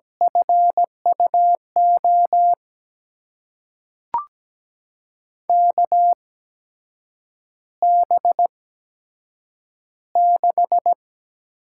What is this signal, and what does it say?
Morse code, heard as a keyed tone, about 17 wpm, spaced as for 7 wpm, 700 Hz. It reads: FUO KB6
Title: USA Call Signs - 6 Characters - 8x Character Spacing - 17wpm